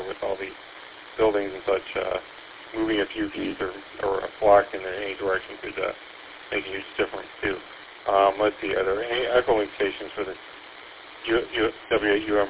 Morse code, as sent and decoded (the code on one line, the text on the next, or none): none